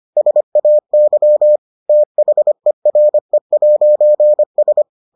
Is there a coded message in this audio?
Yes: Morse code from 0.2 to 4.8 s, heard as a keyed tone, about 25 wpm, 600 Hz. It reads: SAY THERE'S